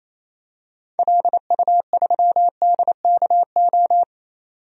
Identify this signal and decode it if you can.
Morse code, heard as a keyed tone, about 27 wpm, 700 Hz. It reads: LU3DKO
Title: DXCC-100 Call Signs - 6 Characters - 28wpm